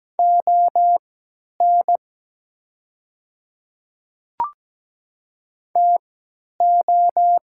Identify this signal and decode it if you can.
Morse code, heard as a keyed tone, about 17 wpm, spaced as for 10 wpm, 700 Hz. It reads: ON TO
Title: 3x Character Spacing - Top 100 Words - 17wpm